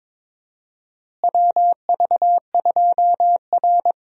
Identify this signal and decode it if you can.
Morse code, heard as a keyed tone, about 22 wpm, 700 Hz. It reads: WV2R
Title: USA Call Signs - 4 Characters - Repeated 2x - 22wpm